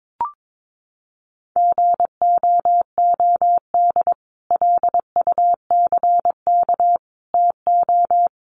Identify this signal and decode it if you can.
Morse code, heard as a keyed tone, about 22 wpm, 700 Hz. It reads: GOOD LUCK TO
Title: Sentences from Top 1500 Words 22wpm